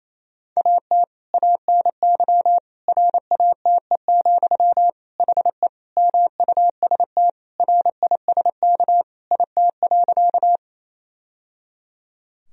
Morse code, heard as a keyed tone, about 28 wpm, 700 Hz. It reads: AT ANY RATE, HE MUST RISK IT.
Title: Sentences from Top 1000 Words - Review Part 4 - 28wpm